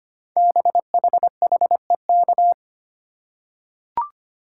Morse code, heard as a keyed tone, about 25 wpm, 700 Hz. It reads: BHHEK
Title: Code Groups - 5 Letters - 25wpm